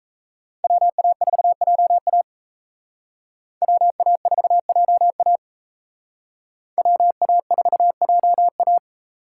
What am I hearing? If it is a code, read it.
Morse code, heard as a keyed tone, about 36 wpm, 700 Hz. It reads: WA4JA WA4JA WA4JA